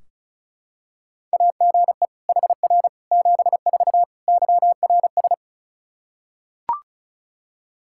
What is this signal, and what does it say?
Morse code, heard as a keyed tone, about 36 wpm, 700 Hz. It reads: AGE HR 74 YRS
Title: Practice QSO - Exchange 3 - More Info - 35wpm